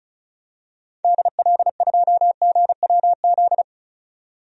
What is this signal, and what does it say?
Morse code, heard as a keyed tone, about 34 wpm, 700 Hz. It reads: DL2GWZ
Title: DXCC-100 Call Signs - 6 Characters - 35wpm